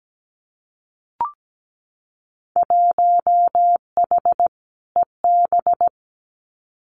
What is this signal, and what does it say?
Morse code, heard as a keyed tone, about 17 wpm, 700 Hz. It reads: 1H EB
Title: Winter Field Day Exchanges 17wpm